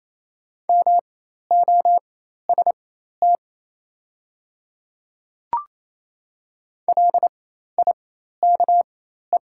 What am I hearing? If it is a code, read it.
Morse code, heard as a keyed tone, about 28 wpm, spaced as for 13 wpm, 700 Hz. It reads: MOST LIKE
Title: 4x Character Spacing - Top 100 Words - 28wpm